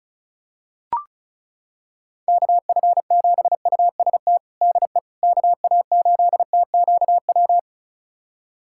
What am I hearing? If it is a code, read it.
Morse code, heard as a keyed tone, about 36 wpm, 700 Hz. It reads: KF7UST DE KA8TQW
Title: USA Callsign DE Callsign 35wpm